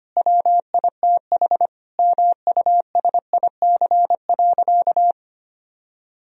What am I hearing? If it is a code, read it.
Morse code, heard as a keyed tone, about 25 wpm, 700 Hz. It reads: WITH MUSIC.
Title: Sentences from Top 1000 Words - Review Part 1 - 25wpm